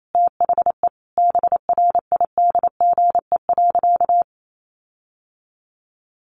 Morse code, heard as a keyed tone, about 28 wpm, 700 Hz. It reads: THE BRIDGE.
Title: Sentences from Top 1500 Words 28wpm